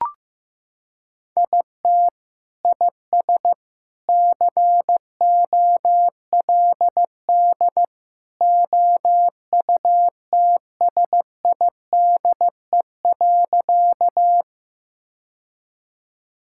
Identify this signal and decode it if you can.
Morse code, heard as a keyed tone, about 15 wpm, 700 Hz. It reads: IT IS COLD OUTSIDE.
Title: Sentences from Top 1000 Words - Review Part 6 - 15wpm